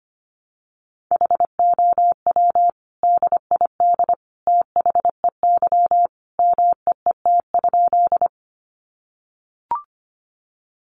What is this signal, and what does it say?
Morse code, heard as a keyed tone, about 25 wpm, 700 Hz. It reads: HOW DID THEY MEET?
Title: Sentences from Top 1100 Words 25wpm